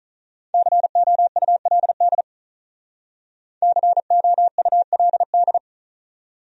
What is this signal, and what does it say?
Morse code, heard as a keyed tone, about 46 wpm, 700 Hz. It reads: COULD COULD